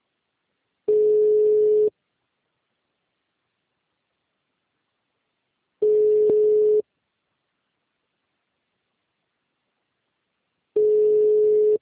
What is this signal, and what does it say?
Ringback tone (Europe, 425 Hz).